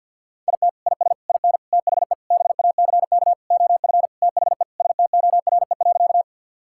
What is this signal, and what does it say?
Morse code, heard as a keyed tone, about 47 wpm, 700 Hz. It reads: IT IS IN THE BACK OF THE STORE.
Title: Sentences from Top 1000 Words - Review Part 5 - 50wpm